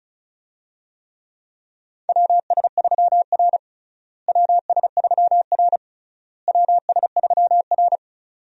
Morse code, heard as a keyed tone, about 36 wpm, 700 Hz. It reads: WS3R WS3R WS3R
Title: USA Call Signs - 4 Characters - Repeated 3x - 35wpm